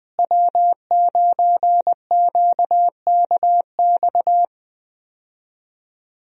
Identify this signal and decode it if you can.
Morse code, heard as a keyed tone, about 20 wpm, 700 Hz. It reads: W9QKX
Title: USA Call Signs - 5 Characters - Repeated 2x - 20wpm